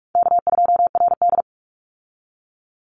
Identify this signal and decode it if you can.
Morse code, heard as a keyed tone, about 44 wpm, 700 Hz. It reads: K2RD
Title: Easy Call Signs Repeated 3x - 45wpm